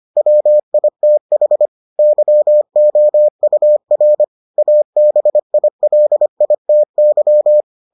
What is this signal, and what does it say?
Morse code, heard as a keyed tone, about 25 wpm, 600 Hz. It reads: WITH YOUR ABILITY